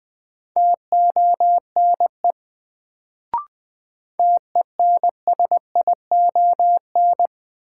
Morse code, heard as a keyed tone, about 20 wpm, 700 Hz. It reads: TONE TENSION
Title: Sentences from Top 1600 Words - New Word Drill Alt - 20wpm